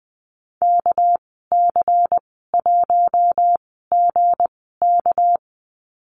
Morse code, heard as a keyed tone, about 20 wpm, spaced as for 14 wpm, 700 Hz. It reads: KC1GK
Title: 219 Lesson 35g - Number 0 - Call Signs 20wpm